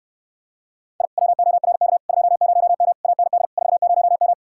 Morse code, heard as a keyed tone, about 69 wpm, 700 Hz. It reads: I WORK 20M AND 40M